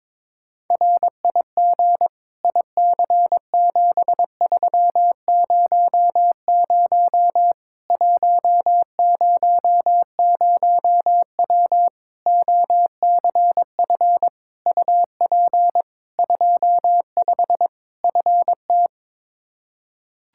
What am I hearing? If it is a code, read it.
Morse code, heard as a keyed tone, about 22 wpm, 700 Hz. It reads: RIG IC7300 100W OCF UP 25 FT
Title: Practice QSO - Exchange 2 - Info - 22wpm